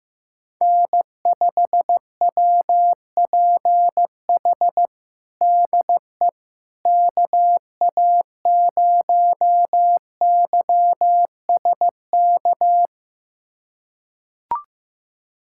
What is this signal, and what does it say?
Morse code, heard as a keyed tone, about 15 wpm, 700 Hz. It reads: N5WPH DE KA0YSK
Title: USA Callsign DE Callsign 15wpm